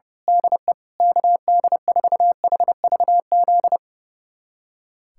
Morse code, heard as a keyed tone, about 30 wpm, 700 Hz. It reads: DE KD4HVZ
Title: USA Callsign DE Callsign 30wpm